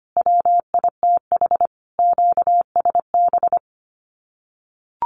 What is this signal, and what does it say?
Morse code, heard as a keyed tone, about 25 wpm, 700 Hz. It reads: WITH QSB
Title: Common QSO Phrases Encore 25wpm